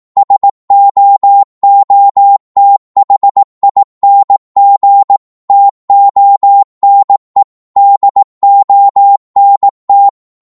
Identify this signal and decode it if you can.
Morse code, heard as a keyed tone, about 18 wpm, 800 Hz. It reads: SOOTHING TONE DONT
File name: Morse / psg-18wpm